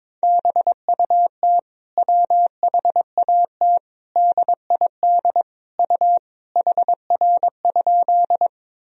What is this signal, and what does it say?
Morse code, heard as a keyed tone, about 22 wpm, 700 Hz. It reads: BUT WHAT DID U HR?
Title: Sentences from Top 300 Words and Abbreviations 22wpm